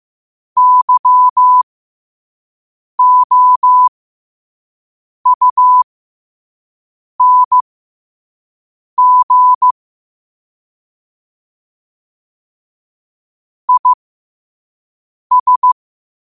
Morse code, heard as a keyed tone, about 15 wpm, spaced as for 5 wpm, 1000 Hz. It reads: YOUNG IS
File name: qotdfiles / 5wpm